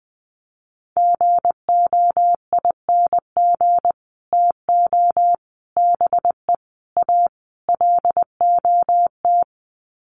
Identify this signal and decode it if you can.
Morse code, heard as a keyed tone, about 20 wpm, 700 Hz. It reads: GOING TO BE A LOT
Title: Sets of 5 Words Encore 20wpm